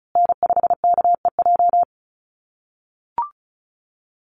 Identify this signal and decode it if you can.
Morse code, heard as a keyed tone, about 34 wpm, 700 Hz. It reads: N5KEJ
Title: USA Call Signs Repeated 2x 35wpm